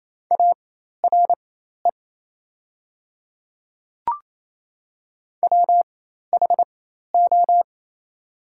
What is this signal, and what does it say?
Morse code, heard as a keyed tone, about 28 wpm, spaced as for 13 wpm, 690 Hz. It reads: ARE WHO
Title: 4x Character Spacing - Top 300 Words - 28wpm